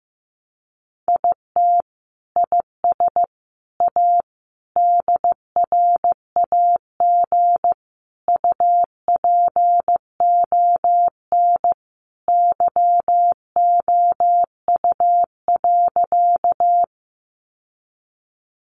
Morse code, heard as a keyed tone, about 15 wpm, 700 Hz. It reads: IT IS A DRAG UPON YOU.
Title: Sentences from Top 1500 Words 15wpm